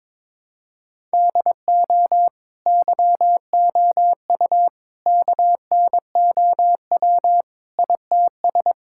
Morse code, heard as a keyed tone, about 22 wpm, 700 Hz. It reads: DO YOU KNOW ITS
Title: Sentences from Top 1000 Words - Review Part 5 - 22wpm